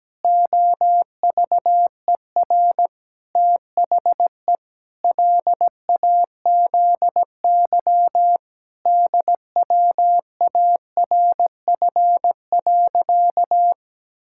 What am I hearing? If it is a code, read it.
Morse code, heard as a keyed tone, about 17 wpm, 700 Hz. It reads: OVER THE LAZY DWARF.
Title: Pangram Challenge v2 - 17wpm